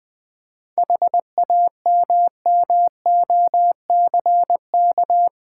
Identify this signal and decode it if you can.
Morse code, heard as a keyed tone, about 20 wpm, 700 Hz. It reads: HAMMOCK
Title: Easy Words with Hints - 20wpm